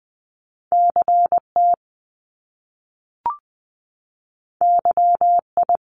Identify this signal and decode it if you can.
Morse code, heard as a keyed tone, about 20 wpm, 700 Hz. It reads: CT YI
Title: Top Two Letter Combinations in Words 20wpm